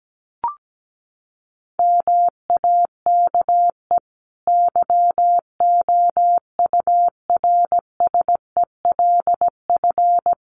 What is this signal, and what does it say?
Morse code, heard as a keyed tone, about 17 wpm, 700 Hz. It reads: MAKE YOURSELF
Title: Sentences from Top 600 Words 17wpm